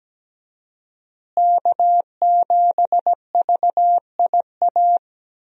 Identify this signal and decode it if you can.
Morse code, heard as a keyed tone, about 17 wpm, 700 Hz. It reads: K7VIA
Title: USA Call Signs - 5 Characters - Repeated 2x - 17wpm